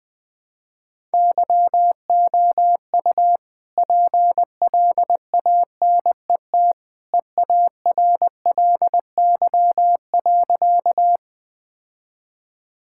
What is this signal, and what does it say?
Morse code, heard as a keyed tone, about 20 wpm, 700 Hz. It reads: YOU PLANET EARLY.
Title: All-Age Appropriate Jokes - 20wpm